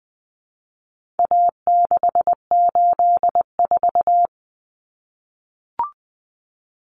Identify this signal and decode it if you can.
Morse code, heard as a keyed tone, about 20 wpm, 700 Hz. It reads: A684
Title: Common Cut Numbers - 4-Digit 20wpm